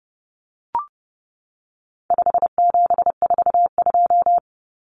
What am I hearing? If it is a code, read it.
Morse code, heard as a keyed tone, about 30 wpm, 700 Hz. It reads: 5742
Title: Common Cut Numbers - 4-Digit 30wpm